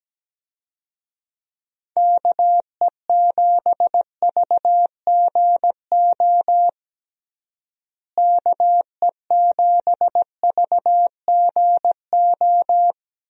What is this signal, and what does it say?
Morse code, heard as a keyed tone, about 17 wpm, 700 Hz. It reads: KE7VGO KE7VGO
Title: USA Call Signs - 6 Characters - Repeated 3x - 17wpm